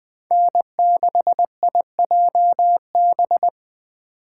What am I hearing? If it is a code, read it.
Morse code, heard as a keyed tone, about 21 wpm, 700 Hz. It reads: N6IJB
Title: Speed Racing - USA 1x3 Call Signs 17wpm